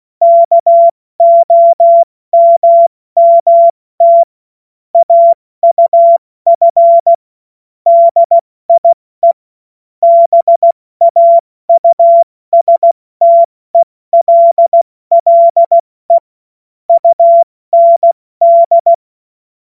Morse code, heard as a keyed tone, about 16 wpm, 680 Hz. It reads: KOMMT AUF DIE BAUSTELLE UND
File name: Morsekurs / Witze / Witz 13. Januar 2026 70 F80